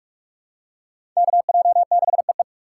Morse code, heard as a keyed tone, about 42 wpm, 700 Hz. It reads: KJ6EE
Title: USA Call Signs - 5 Characters - 2x Character Spacing - 45wpm